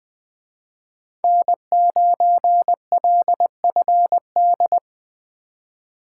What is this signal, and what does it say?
Morse code, heard as a keyed tone, about 21 wpm, 700 Hz. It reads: N9LFD
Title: Speed Racing - USA 1x3 Call Signs 17wpm